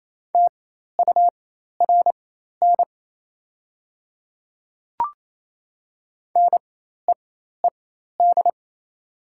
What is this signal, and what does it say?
Morse code, heard as a keyed tone, about 27 wpm, spaced as for 13 wpm, 700 Hz. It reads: TURN NEED
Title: 4x Character Spacing - Top 200 Words - 28wpm